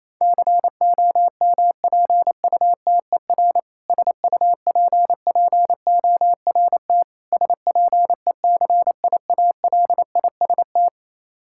Morse code, heard as a keyed tone, about 28 wpm, 700 Hz. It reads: COMPUTER SUPPORT SPECIALIST
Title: Practice QSO - Exchange 3 - More Info - 28wpm